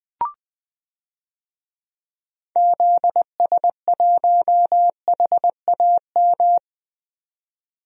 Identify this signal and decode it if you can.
Morse code, heard as a keyed tone, about 20 wpm, 700 Hz. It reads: ZS1HAM
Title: DXCC-100 Call Signs - 6 Characters - 20wpm